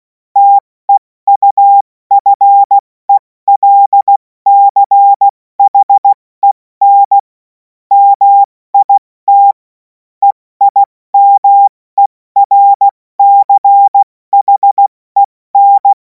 Morse code, heard as a keyed tone, about 16 wpm, 800 Hz. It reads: TEUFELCHEN MIT EIMERCHEN
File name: Morsekurs / Witze / Witz 03. März 2026 70 F80